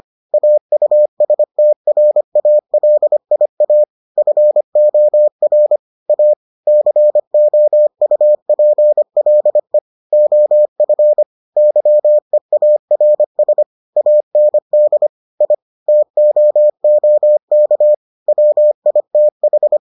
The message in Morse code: AUSTRALIA FOR A COUPLE OF YEARS AND I TOOK WITH

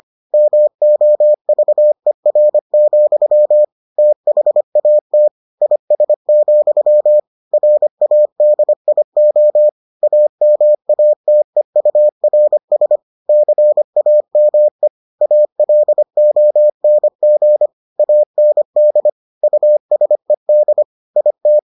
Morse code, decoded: MOVER, THAT IS, RADIO AMATEURS CAME ALONG AND USED IT

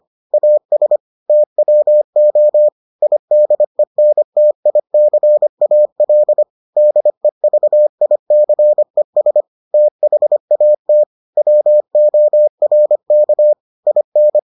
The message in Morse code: AS TWO IDENTICAL DEVICES THAT WORK IN

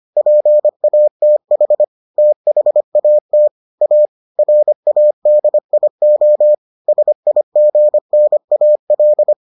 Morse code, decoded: PATH THAT A RADIO SIGNAL

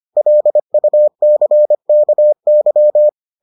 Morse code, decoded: LUCKY

